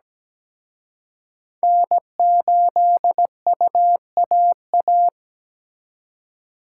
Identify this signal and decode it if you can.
Morse code, heard as a keyed tone, about 20 wpm, 700 Hz. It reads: N8UAA
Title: Speed Racing - USA 1x3 Call Signs 17wpm